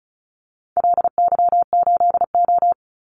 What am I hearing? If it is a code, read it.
Morse code, heard as a keyed tone, about 34 wpm, 700 Hz. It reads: LY8O
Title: Call Signs Repeated 2x - 35wpm